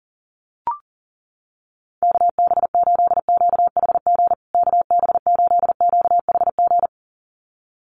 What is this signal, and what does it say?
Morse code, heard as a keyed tone, about 40 wpm, 700 Hz. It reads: KB8QHG KB8QHG
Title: USA Call Signs Repeated 2x 40wpm